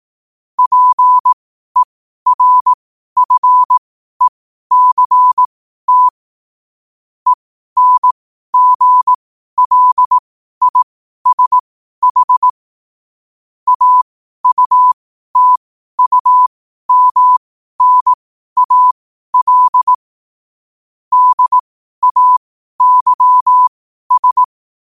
Morse code, heard as a keyed tone, about 18 wpm, spaced as for 12 wpm, 1000 Hz. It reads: PERFECT ENGLISH AUTUMNAL DAYS
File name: qotdfiles / 12wpm